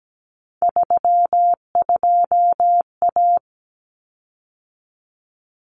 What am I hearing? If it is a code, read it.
Morse code, heard as a keyed tone, about 17 wpm, 700 Hz. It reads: 32A
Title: Common Cut Numbers - 3-Digit 17wpm